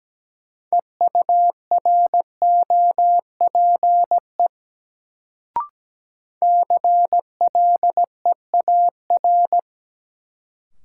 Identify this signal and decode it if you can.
Morse code, heard as a keyed tone, about 17 wpm, 700 Hz. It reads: EUROPE CLEAR
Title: Sets of 5 Words Encore - New Word Drill - 17wpm